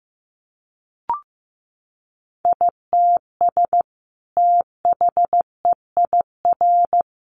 Morse code, heard as a keyed tone, about 15 wpm, 700 Hz. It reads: ITS THEIR
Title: Sentences from Top 900 Words 15wpm